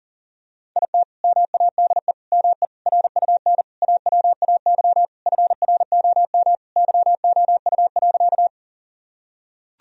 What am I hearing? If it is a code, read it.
Morse code, heard as a keyed tone, about 40 wpm, 700 Hz. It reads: IT MADE ME RUN AWAY FROM YOU.
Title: Sentences from Top 1000 Words - Review Part 1 - 40wpm